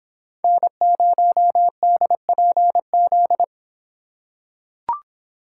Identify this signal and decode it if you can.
Morse code, heard as a keyed tone, about 25 wpm, 700 Hz. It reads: N0DPZ